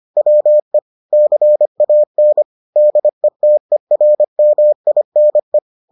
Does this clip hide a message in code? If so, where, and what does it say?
Yes: Morse code from 0.2 to 5.6 s, heard as a keyed tone, about 25 wpm, 600 Hz. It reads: WE CAN DETERMINE